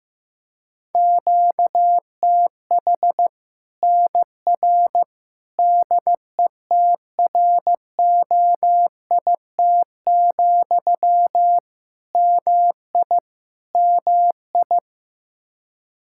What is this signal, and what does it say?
Morse code, heard as a keyed tone, about 15 wpm, 700 Hz. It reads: QTH NR DETROIT, MI MI
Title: Practice QSO - Exchange 1 - Beginning - 15wpm